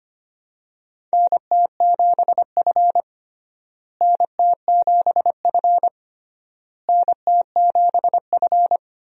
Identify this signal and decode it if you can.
Morse code, heard as a keyed tone, about 25 wpm, 700 Hz. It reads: NT7F NT7F NT7F
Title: USA Call Signs - 4 Characters - Repeated 3x - 25wpm